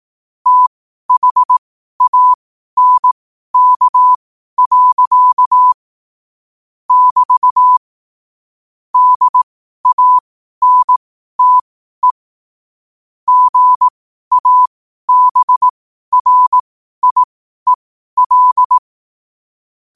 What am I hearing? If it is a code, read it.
Morse code, heard as a keyed tone, about 18 wpm, spaced as for 12 wpm, 1000 Hz. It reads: THANK. <BT> DANTE GABRIEL
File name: qotdfiles / 12wpm